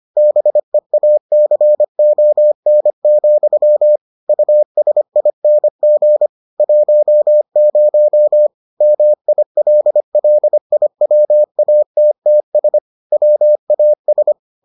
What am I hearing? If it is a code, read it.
Morse code, heard as a keyed tone, about 25 wpm, 600 Hz. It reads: BEACON, USING 10 MILLIWATTS WAS